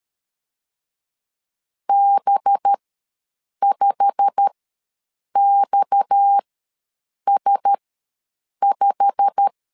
Morse code, heard as a keyed tone, about 13 wpm, spaced as for 7 wpm, 800 Hz. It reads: B5XS5
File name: LEZIONI / 8 YJ45 / 8 YJ45 ws 12wpm 3146